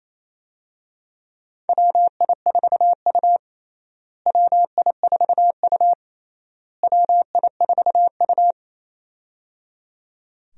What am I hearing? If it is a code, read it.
Morse code, heard as a keyed tone, about 28 wpm, 700 Hz. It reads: WI4U WI4U WI4U
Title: USA Call Signs - 4 Characters - Repeated 3x - 28wpm